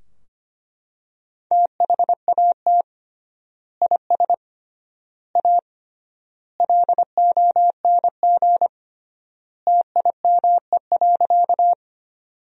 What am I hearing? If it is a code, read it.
Morse code, heard as a keyed tone, about 25 wpm, 700 Hz. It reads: THAT IS A LONG TIME.
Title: 3x Word-Spacing - Sentences from Top 100 Words - 25wpm